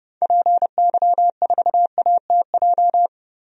Morse code, heard as a keyed tone, about 31 wpm, 700 Hz. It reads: PY4ATJ
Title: DXCC-100 Call Signs - 6 Characters - 30wpm